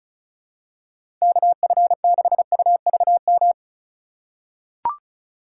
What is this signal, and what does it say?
Morse code, heard as a keyed tone, about 36 wpm, 700 Hz. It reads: KF6UVM
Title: USA Call Signs - 6 Characters - 3x Character Spacing - 35wpm